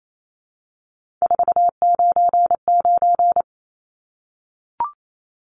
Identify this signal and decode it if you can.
Morse code, heard as a keyed tone, about 28 wpm, 700 Hz. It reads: 499
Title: Numbers - 3-Digit 28wpm